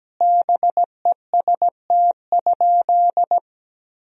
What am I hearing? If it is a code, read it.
Morse code, heard as a keyed tone, about 17 wpm, 700 Hz. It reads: BEST?
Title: Sentences from Top 1000 Words - Review Part 1 - 17wpm